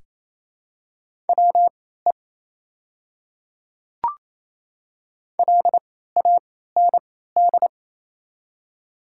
Morse code, heard as a keyed tone, about 27 wpm, spaced as for 16 wpm, 700 Hz. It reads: WE LAND
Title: 3x Character Spacing - Top 200 Words - 28wpm